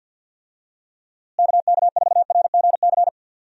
Morse code, heard as a keyed tone, about 51 wpm, 700 Hz. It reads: KK4RGC
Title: USA Call Signs - 6 Characters - Repeated 3x - 50wpm